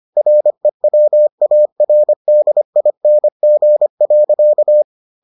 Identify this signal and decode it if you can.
Morse code, heard as a keyed tone, about 25 wpm, 600 Hz. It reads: REWARDING.